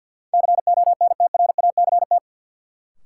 Morse code, heard as a keyed tone, about 52 wpm, 700 Hz. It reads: CONTRACT
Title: Sentences from Top 1400 Words - New Word Drill Alt - 50wpm